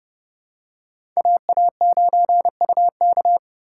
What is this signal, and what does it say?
Morse code, heard as a keyed tone, about 30 wpm, 700 Hz. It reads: AA9UK
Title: USA Call Signs - 5 Characters - 4x Character Spacing - 30wpm